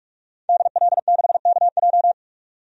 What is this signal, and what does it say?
Morse code, heard as a keyed tone, about 46 wpm, 700 Hz. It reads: DL6KJ
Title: DXCC-100 Call Signs - 5 Characters - 45wpm